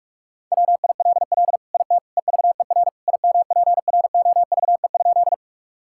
Morse code, heard as a keyed tone, about 42 wpm, 700 Hz. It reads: WILL IT EVER IMPROVE?
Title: Sentences from Top 1000 Words - Review Part 5 - 45wpm